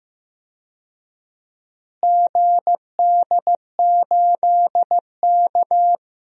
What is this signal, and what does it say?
Morse code, heard as a keyed tone, about 16 wpm, 700 Hz. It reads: GD8K
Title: DXCC-100 Call Signs - 4 Characters - Repeated 3x - 15wpm